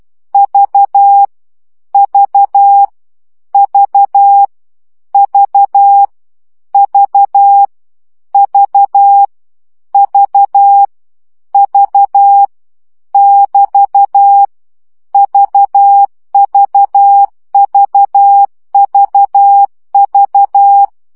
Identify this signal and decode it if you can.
Morse code, heard as a keyed tone, about 12 wpm, 800 Hz. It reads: V V V V V V V V <BT> VVVVV
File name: CW gyakorlas v 10 baud